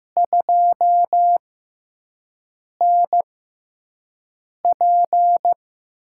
Morse code, heard as a keyed tone, about 15 wpm, spaced as for 5 wpm, 700 Hz. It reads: 2NP
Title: USA Call Signs - 4 Characters - 6x Character Spacing - 15wpm